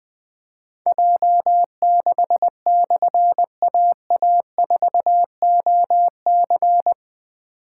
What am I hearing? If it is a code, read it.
Morse code, heard as a keyed tone, about 20 wpm, 700 Hz. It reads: J6/AA4OC
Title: Difficult Call Signs 20wpm